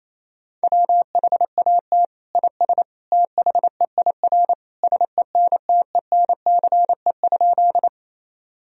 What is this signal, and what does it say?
Morse code, heard as a keyed tone, about 28 wpm, 700 Hz. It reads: WHAT IS THEIR SENTENCE?
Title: Sentences from Top 1000 Words - Review Part 1 - 28wpm